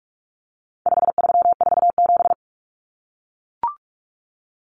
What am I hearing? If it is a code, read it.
Morse code, heard as a keyed tone, about 44 wpm, 700 Hz. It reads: 5347
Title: Common Cut Numbers - 4-Digit 45wpm